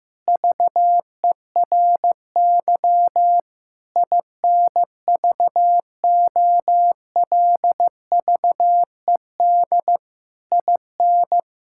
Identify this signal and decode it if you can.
Morse code, heard as a keyed tone, about 15 wpm, 700 Hz. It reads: VERY INVOLVED IN
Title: Sentences from Top 1000 Words - Review Part 5 - 15wpm